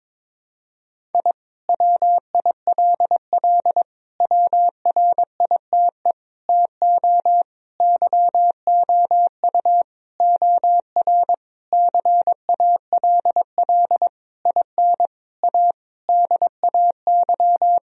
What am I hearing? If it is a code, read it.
Morse code, heard as a keyed tone, about 22 wpm, 700 Hz. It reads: I WILL WRITE TO YOU OR CALL IN A DAY
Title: Sentences from Top 1000 Words - Review Part 1 - 22wpm